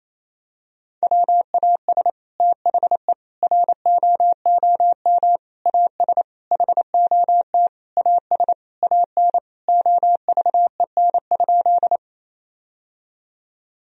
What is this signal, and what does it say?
Morse code, heard as a keyed tone, about 28 wpm, 700 Hz. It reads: WAS THE ROOM AS HOT AS AN OVEN?
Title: Sentences from Top 1000 Words - Review Part 4 - 28wpm